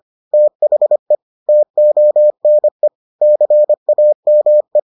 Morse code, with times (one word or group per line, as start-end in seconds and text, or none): 0.3-1.1 THE
1.5-2.9 TONE
3.2-4.8 CAME